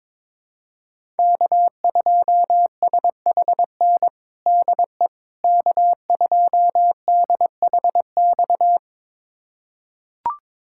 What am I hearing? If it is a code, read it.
Morse code, heard as a keyed tone, about 21 wpm, 700 Hz. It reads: K2SHN DE K2DHX
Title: USA Callsign DE Callsign - 5 Characters - 22wpm